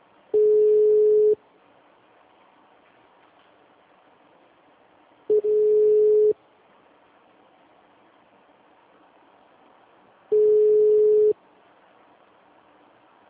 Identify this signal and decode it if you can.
Ringback tone (Europe, 425 Hz).
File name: out- -2011- - - .36